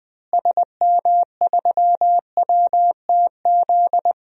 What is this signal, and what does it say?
Morse code, heard as a keyed tone, about 20 wpm, 700 Hz. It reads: SM3WTZ
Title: DXCC-100 Call Signs - 6 Characters - Repeated 2x - 20wpm